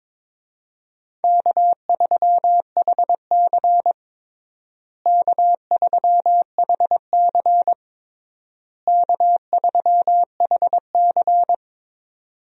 Morse code, heard as a keyed tone, about 22 wpm, 700 Hz. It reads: K3HC K3HC K3HC
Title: USA Call Signs - 4 Characters - Repeated 3x - 22wpm